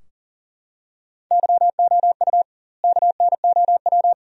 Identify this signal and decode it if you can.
Morse code, heard as a keyed tone, about 40 wpm, 700 Hz. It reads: YOU KNOW
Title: 2x Word-Spacing - Sentences from Top 100 Words - 40wpm